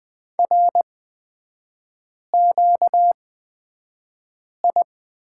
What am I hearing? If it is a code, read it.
Morse code, heard as a keyed tone, about 20 wpm, spaced as for 5 wpm, 700 Hz. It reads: RQI
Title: 236 Lesson 38e - Letter q - Three Characters 20wpm